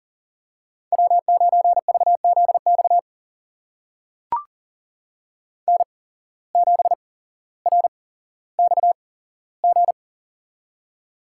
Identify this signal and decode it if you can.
Morse code, heard as a keyed tone, about 40 wpm, spaced as for 17 wpm, 700 Hz. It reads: W9VZX N7RXG
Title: USA Call Signs - 5 Characters - 8x Character Spacing - 40wpm